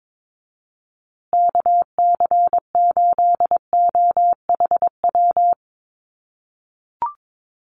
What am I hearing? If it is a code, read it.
Morse code, heard as a keyed tone, about 22 wpm, 700 Hz. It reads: KC8OHW
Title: USA Call Signs Repeated 3x 22wpm